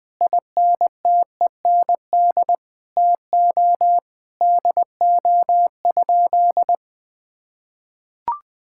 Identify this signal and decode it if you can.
Morse code, heard as a keyed tone, about 20 wpm, 700 Hz. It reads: INTEND TO DO?
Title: Sentences from Top 1600 Words 20wpm